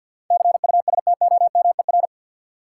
Morse code, heard as a keyed tone, about 52 wpm, 700 Hz. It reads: CUSTOMER
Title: Sentences from Top 1100 Words - New Word Drill Alt - 50wpm